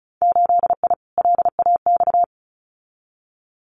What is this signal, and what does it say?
Morse code, heard as a keyed tone, about 36 wpm, 700 Hz. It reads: 8I LAX
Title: Winter Field Day Exchanges Encore 35wpm